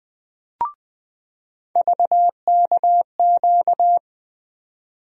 Morse code, heard as a keyed tone, about 20 wpm, 700 Hz. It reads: VKQ
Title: USA Call Sign Postfixes - 3 Letters 20wpm